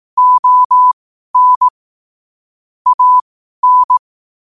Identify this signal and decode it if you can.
Morse code, heard as a keyed tone, about 18 wpm, spaced as for 12 wpm, 1000 Hz. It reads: ON AN
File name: qotdfiles / 12wpm